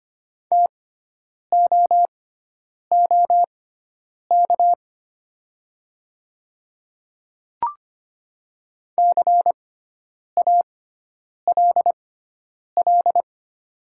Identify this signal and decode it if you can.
Morse code, heard as a keyed tone, about 25 wpm, spaced as for 9 wpm, 700 Hz. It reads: TOOK CALL
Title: 6x Character Spacing - Top 300 Words - 25wpm